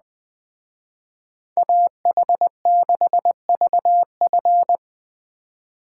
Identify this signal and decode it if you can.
Morse code, heard as a keyed tone, about 20 wpm, 700 Hz. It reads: AH6VF
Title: Speed Racing - USA Call Signs 17wpm